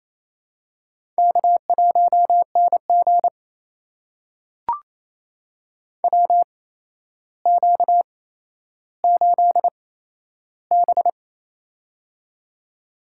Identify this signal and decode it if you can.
Morse code, heard as a keyed tone, about 28 wpm, spaced as for 12 wpm, 700 Hz. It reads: K1NG WQ8B
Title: USA Call Signs - 4 Characters - 8x Character Spacing - 28wpm